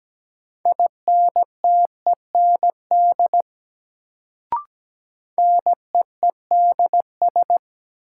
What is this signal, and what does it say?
Morse code, heard as a keyed tone, about 17 wpm, 700 Hz. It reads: INTEND NEEDS
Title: Sentences from Top 1600 Words - New Word Drill Alt - 17wpm